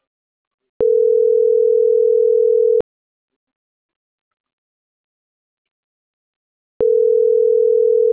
Ringback tone (North America), 440 + 480 Hz.